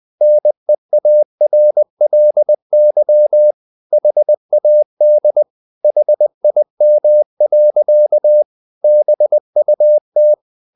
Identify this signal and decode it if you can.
Morse code, heard as a keyed tone, about 20 wpm, 600 Hz. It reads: NEARLY HAD HIM. BUT
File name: the old man and the sea part 2 20wpm